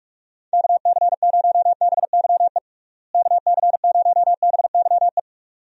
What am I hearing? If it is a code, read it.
Morse code, heard as a keyed tone, about 46 wpm, 700 Hz. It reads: KC0BYE KC0BYE
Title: USA Call Signs - 6 Characters - Repeated 2x - 45wpm